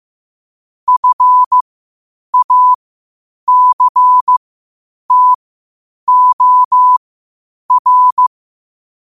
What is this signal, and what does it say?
Morse code, heard as a keyed tone, about 15 wpm, spaced as for 8 wpm, 1000 Hz. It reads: FACTOR
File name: qotdfiles / 8wpm